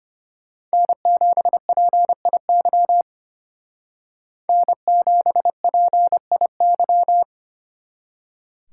Morse code, heard as a keyed tone, about 30 wpm, 700 Hz. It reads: N7PIY N7PIY
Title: Speed Racing - USA 1x3 Call Signs 25wpm